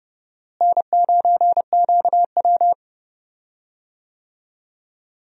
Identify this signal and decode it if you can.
Morse code, heard as a keyed tone, about 30 wpm, 700 Hz. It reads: N9QW
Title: USA Call Signs - 4 Characters - Repeated 2x - 30wpm